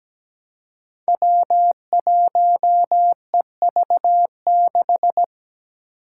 Morse code, heard as a keyed tone, about 17 wpm, 700 Hz. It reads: W1EV6
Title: Five Challenging Letter-Number Combinations 17wpm